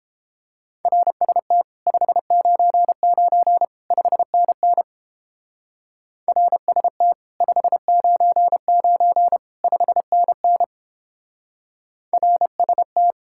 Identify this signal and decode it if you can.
Morse code, heard as a keyed tone, about 29 wpm, 700 Hz. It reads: RST 599 5NN RST 599 5NN RST